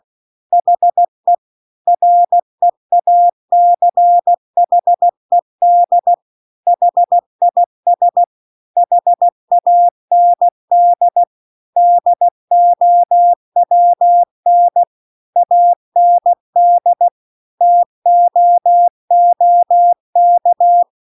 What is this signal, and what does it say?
Morse code, heard as a keyed tone, about 16 wpm, 700 Hz. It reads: HE REACHED HIS HAND DOWN AND TOOK